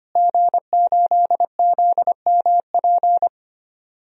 Morse code, heard as a keyed tone, about 25 wpm, 700 Hz. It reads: G8ZMP
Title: DXCC-100 Call Signs - 5 Characters - Repeated 3x - 25wpm